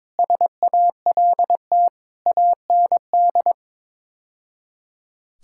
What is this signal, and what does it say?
Morse code, heard as a keyed tone, about 22 wpm, 700 Hz. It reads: SALT AND
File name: practice / 22wpm / Build A Phrase - 5-Word Phrases - 22wpm